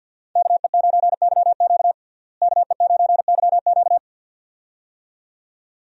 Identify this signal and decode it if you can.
Morse code, heard as a keyed tone, about 50 wpm, 700 Hz. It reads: KE9YQ KE9YQ
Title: USA Call Signs - 5 Characters - Repeated 2x - 50wpm